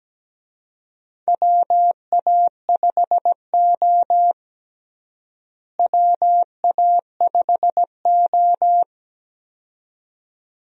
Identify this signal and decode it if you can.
Morse code, heard as a keyed tone, about 17 wpm, 700 Hz. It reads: WA5O WA5O
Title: USA Call Signs - 4 Characters - Repeated 2x - 17wpm